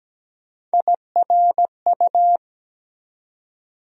Morse code, heard as a keyed tone, about 17 wpm, 700 Hz. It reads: IRU
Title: USA Call Sign Postfixes - 3 Letters 17wpm